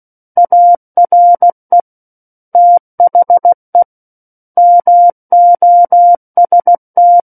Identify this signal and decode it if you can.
Morse code, heard as a keyed tone, about 16 wpm, 700 Hz. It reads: ARE THE MOST
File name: Book One / 07-Chapter 7 - 16 WPM